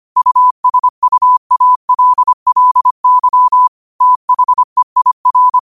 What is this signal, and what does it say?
Morse code, heard as a keyed tone, about 25 wpm, 1000 Hz. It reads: USUALLY THEIR
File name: qotdfiles / 25wpm